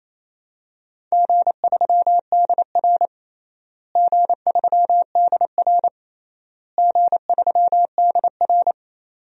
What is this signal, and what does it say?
Morse code, heard as a keyed tone, about 28 wpm, 700 Hz. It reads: G3DR G3DR G3DR
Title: DXCC-100 Call Signs - 4 Characters - Repeated 3x - 28wpm